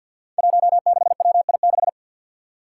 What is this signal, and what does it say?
Morse code, heard as a keyed tone, about 53 wpm, 700 Hz. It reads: 16WI6
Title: Five Challenging Letter-Number Combinations 50wpm